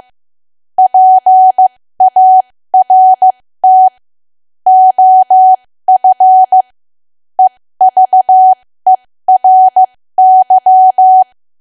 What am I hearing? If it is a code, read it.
Morse code, heard as a keyed tone, about 15 wpm, spaced as for 13 wpm, 750 Hz. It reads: PART OF EVERY
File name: cwa / W1AW-Level2 / 13WPM